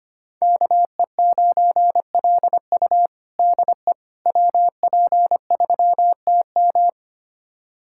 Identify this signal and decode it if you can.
Morse code, heard as a keyed tone, about 25 wpm, 700 Hz. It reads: KE9LU DE WP3TM
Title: USA Callsign DE Callsign - 5 Characters - 25wpm